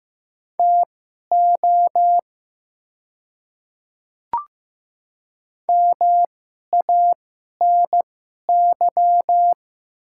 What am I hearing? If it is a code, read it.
Morse code, heard as a keyed tone, about 15 wpm, spaced as for 11 wpm, 700 Hz. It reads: TO MANY
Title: 2x Character Spacing - Top 100 Words - 15wpm